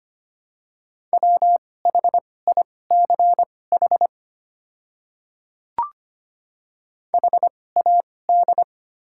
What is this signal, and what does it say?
Morse code, heard as a keyed tone, about 25 wpm, spaced as for 18 wpm, 700 Hz. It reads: WHICH HAD
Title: 2x Character Spacing - Top 300 Words - 25wpm